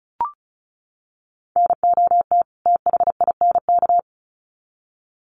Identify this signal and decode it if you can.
Morse code, heard as a keyed tone, about 34 wpm, 700 Hz. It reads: NOT THINK
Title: Sets of 2 Words 35wpm